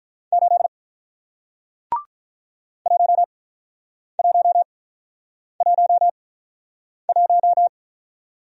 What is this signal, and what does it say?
Morse code, heard as a keyed tone, about 48 wpm, spaced as for 9 wpm, 700 Hz. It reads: 8 1111